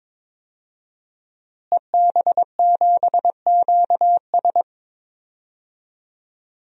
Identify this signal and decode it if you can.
Morse code, heard as a keyed tone, about 21 wpm, 700 Hz. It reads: EB7QS
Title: DXCC-100 Call Signs - 5 Characters - Repeated 3x - 22wpm